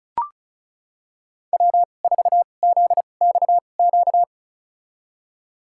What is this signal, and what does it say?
Morse code, heard as a keyed tone, about 36 wpm, spaced as for 25 wpm, 700 Hz. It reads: W4ZXQ
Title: USA Call Signs - 5 Characters - 2x Character Spacing - 35wpm